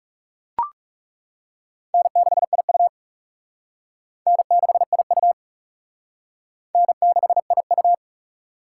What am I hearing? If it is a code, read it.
Morse code, heard as a keyed tone, about 35 wpm, 700 Hz. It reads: N6IU N6IU N6IU
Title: Speed Racing - USA 1x2 Call Signs 30wpm